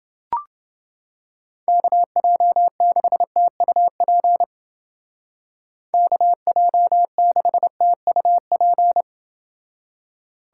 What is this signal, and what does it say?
Morse code, heard as a keyed tone, about 29 wpm, 700 Hz. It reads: KJ6TUP KJ6TUP